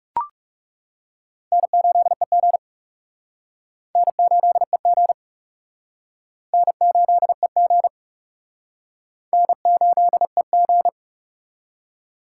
Morse code, heard as a keyed tone, about 40 wpm, 700 Hz. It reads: N8EG N8EG N8EG N8EG